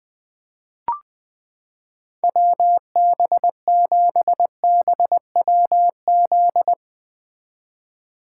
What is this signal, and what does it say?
Morse code, heard as a keyed tone, about 20 wpm, 700 Hz. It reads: WB7BWZ
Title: 2023 SOTA Chaser Call Signs - 20wpm